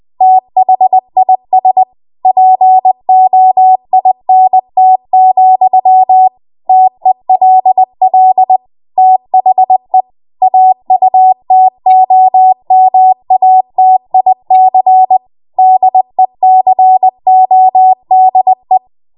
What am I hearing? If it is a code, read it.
Morse code, heard as a keyed tone, about 20 wpm, 750 Hz. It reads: THIS POINT, TELL THE AUTOMATIC DECODE
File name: Morse / 20WPM